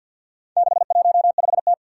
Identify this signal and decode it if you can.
Morse code, heard as a keyed tone, about 51 wpm, 700 Hz. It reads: 615T
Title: Four Challenging Letter-Number Combinations 50wpm